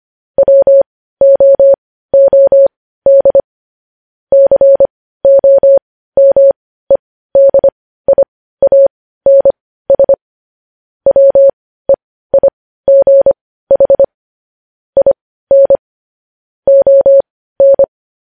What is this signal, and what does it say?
Morse code, heard as a keyed tone, about 25 wpm, spaced as for 15 wpm, 550 Hz. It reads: WOOD COMEDIANS WEIGH IN ON
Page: news headlines in morse code at 15 wpm